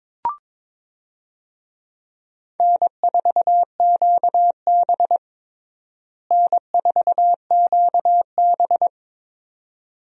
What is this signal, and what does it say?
Morse code, heard as a keyed tone, about 21 wpm, 700 Hz. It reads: N4QB N4QB
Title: USA Call Signs - 4 Characters - Repeated 2x - 22wpm